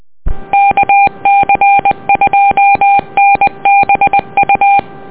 Morse code, heard as a keyed tone, about 20 wpm, 800 Hz. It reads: KC2NBU
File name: net audio